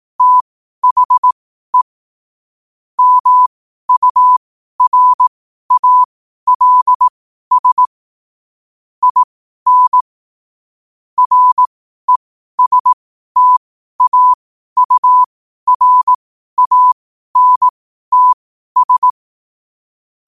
Morse code, heard as a keyed tone, about 18 wpm, spaced as for 12 wpm, 1000 Hz. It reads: THE MURALS IN RESTAURANTS
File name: qotdfiles / 12wpm